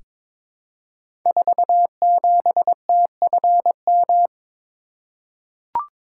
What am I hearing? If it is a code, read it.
Morse code, heard as a keyed tone, about 23 wpm, 700 Hz. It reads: 47TFM
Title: Code Groups - 5 Letter:Numbers - 22wpm